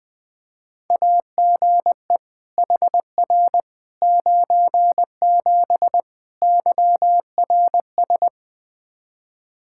Morse code, heard as a keyed tone, about 21 wpm, 700 Hz. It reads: AGE HR 97 YRS
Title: Practice QSO - Exchange 3 - More Info - 20wpm